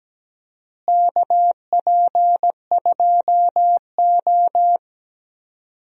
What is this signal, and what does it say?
Morse code, heard as a keyed tone, about 17 wpm, 700 Hz. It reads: KP2O
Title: DXCC-100 Call Signs - 4 Characters - Repeated 3x - 17wpm